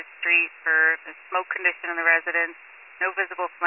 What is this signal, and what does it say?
Emergency Alert System (SAME) tones, heard as harsh buzzing bursts, with speech. It is heard from 0.5 to 3.3 s.